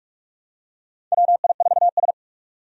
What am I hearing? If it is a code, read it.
Morse code, heard as a keyed tone, about 44 wpm, 700 Hz. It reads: WI4S
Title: USA Call Signs - 4 Characters - 45wpm